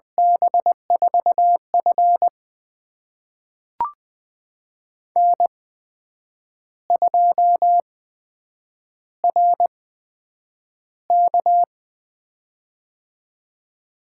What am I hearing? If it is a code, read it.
Morse code, heard as a keyed tone, about 20 wpm, spaced as for 8 wpm, 700 Hz. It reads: B4F N2RK
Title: USA Call Signs - 4 Characters - 8x Character Spacing - 20wpm